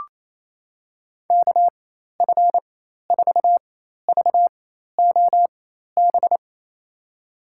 Morse code, heard as a keyed tone, about 27 wpm, spaced as for 13 wpm, 700 Hz. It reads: KF4VOB
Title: USA Call Signs - 6 Characters - 4x Character Spacing - 28wpm